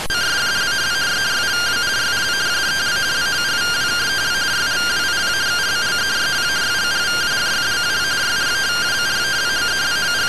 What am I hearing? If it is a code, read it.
Two-tone teleprinter signal, about 170 Hz shift.